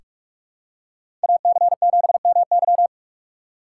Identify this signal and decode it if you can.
Morse code, heard as a keyed tone, about 43 wpm, 700 Hz. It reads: AC7MY
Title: USA Call Signs - 2x Character Spacing - 45wpm